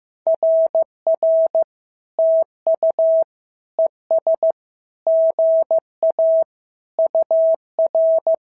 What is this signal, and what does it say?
Morse code, heard as a keyed tone, about 15 wpm, 650 Hz. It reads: RR TU ES GA UR
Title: SOTA QSO Practice 15wpm